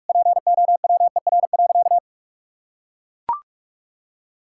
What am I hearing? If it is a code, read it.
Morse code, heard as a keyed tone, about 41 wpm, 700 Hz. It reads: POWER.
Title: Sentences from Top 1000 Words - Review Part 6 - 45wpm